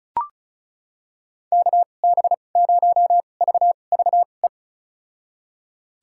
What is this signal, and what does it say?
Morse code, heard as a keyed tone, about 36 wpm, spaced as for 25 wpm, 700 Hz. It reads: KB0VVE
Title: USA Call Signs - 6 Characters - 2x Character Spacing - 35wpm